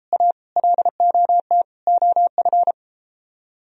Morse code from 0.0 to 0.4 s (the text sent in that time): A